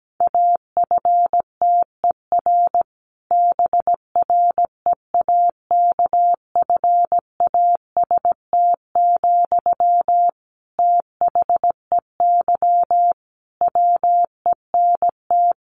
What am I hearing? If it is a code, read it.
Morse code, heard as a keyed tone, about 17 wpm, 700 Hz. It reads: AFTER BREAKFAST, THEY WENT
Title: Sentences from Top 1500 Words 17wpm